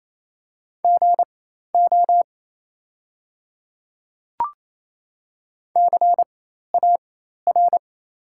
Morse code, heard as a keyed tone, about 27 wpm, spaced as for 13 wpm, 700 Hz. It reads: GO CAR
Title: 4x Character Spacing - Top 300 Words - 28wpm